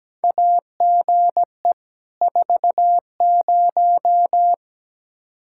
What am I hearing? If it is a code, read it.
Morse code, heard as a keyed tone, about 17 wpm, 700 Hz. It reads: AGE 40
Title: Practice QSO - Exchange 3 - More Info - 17wpm